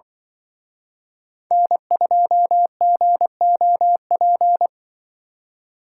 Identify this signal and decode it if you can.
Morse code, heard as a keyed tone, about 24 wpm, 700 Hz. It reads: N2GOP
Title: Speed Racing - USA Call Signs 20wpm